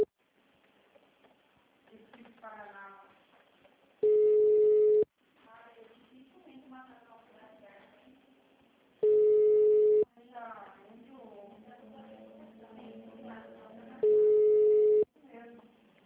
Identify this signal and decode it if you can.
Ringback tone (Europe, 425 Hz), with speech.